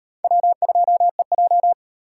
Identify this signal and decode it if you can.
Morse code, heard as a keyed tone, about 36 wpm, 700 Hz. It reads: W2EJ